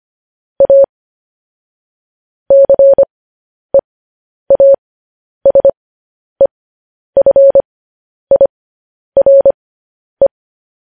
Morse code, heard as a keyed tone, about 25 wpm, spaced as for 10 wpm, 550 Hz. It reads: A CEASEFIRE
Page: news-headlines-in-morse-code-at-10-wpm